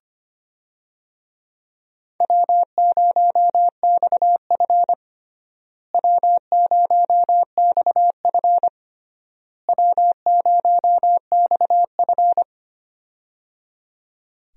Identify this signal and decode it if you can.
Morse code, heard as a keyed tone, about 25 wpm, 700 Hz. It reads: W0XF W0XF W0XF
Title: USA Call Signs - 4 Characters - Repeated 3x - 25wpm